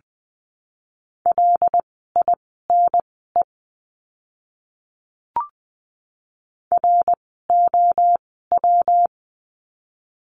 Morse code, heard as a keyed tone, about 20 wpm, spaced as for 15 wpm, 700 Hz. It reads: LINE ROW
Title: 115 Lesson 19f - Letter w - Words 20wpm